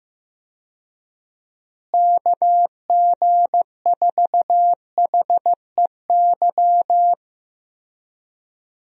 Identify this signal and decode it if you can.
Morse code, heard as a keyed tone, about 15 wpm, 700 Hz. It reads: KG4HEY
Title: USA Call Signs - 6 Characters - Repeated 3x - 15wpm